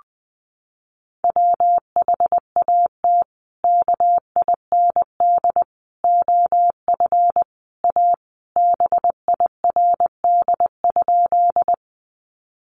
Morse code, heard as a keyed tone, about 20 wpm, 700 Hz. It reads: WHAT KIND OF A BIRD?
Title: Sentences from Top 400 Words 20wpm